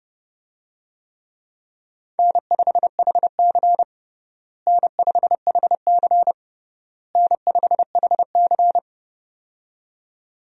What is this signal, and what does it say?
Morse code, heard as a keyed tone, about 30 wpm, 700 Hz. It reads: N5HC N5HC N5HC
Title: USA Call Signs - 4 Characters - Repeated 3x - 30wpm